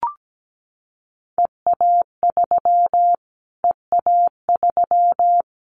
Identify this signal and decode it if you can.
Morse code, heard as a keyed tone, about 17 wpm, 700 Hz. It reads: EA3 EA3
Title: Callsign Prefixes for Top 25 DXCC - Repeated 3x - 17wpm